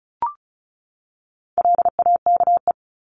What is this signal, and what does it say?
Morse code, heard as a keyed tone, about 34 wpm, 700 Hz. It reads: LAKE
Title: Top 1001-1250 Words 35wpm